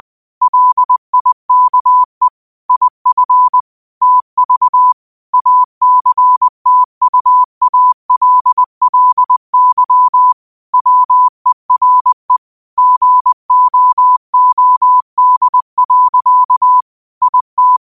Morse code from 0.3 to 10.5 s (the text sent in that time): LIKE IF TV ACTUALLY